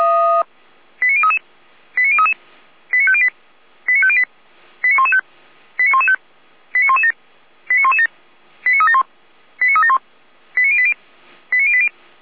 Five-tone selective call (ZVEI), heard as quick runs of five tones, tones of about 70 ms. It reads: 89033 89033 89508 89508 89115 89115 89117 89117 89371 89371 89080 89080
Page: /455/SDR/ZVEI/89120/2025.07.28/